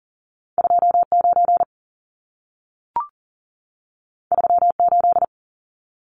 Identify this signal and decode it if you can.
Morse code, heard as a keyed tone, about 40 wpm, 700 Hz. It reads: 29 38
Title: Numbers - 2-Digit 40wpm